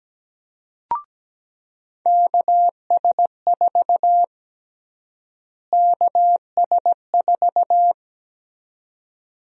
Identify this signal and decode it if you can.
Morse code, heard as a keyed tone, about 17 wpm, 700 Hz. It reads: KS4 KS4
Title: USA Call Sign Prefixes - Repeated 2x - 17wpm